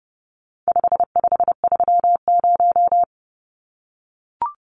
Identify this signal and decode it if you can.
Morse code, heard as a keyed tone, about 30 wpm, 700 Hz. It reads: 5530
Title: Numbers - 4-Digit 30wpm